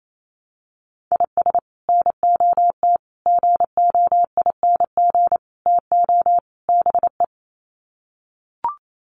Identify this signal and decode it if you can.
Morse code, heard as a keyed tone, about 28 wpm, 700 Hz. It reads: IS NOT GOING TO BE
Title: Sets of 5 Words Encore2 28wpm